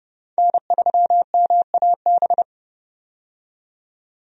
Morse code, heard as a keyed tone, about 32 wpm, 700 Hz. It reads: N3MAB
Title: USA Call Signs - 5 Characters - 30wpm